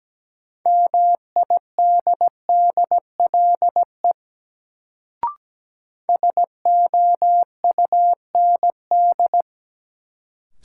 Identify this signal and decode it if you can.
Morse code, heard as a keyed tone, about 17 wpm, 700 Hz. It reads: MIDDLE SOUND
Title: Sentences from Top 900 Words - New Word Drill - 17wpm